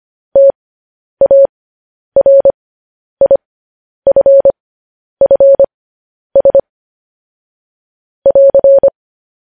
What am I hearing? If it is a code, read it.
Morse code, heard as a keyed tone, about 25 wpm, spaced as for 10 wpm, 550 Hz. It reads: TARIFFS <AR>
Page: News Headlines in Morse Code at 10 WPM